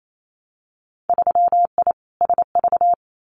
Morse code, heard as a keyed tone, about 28 wpm, 700 Hz. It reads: 3I SV
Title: Winter Field Day Exchanges Encore 28wpm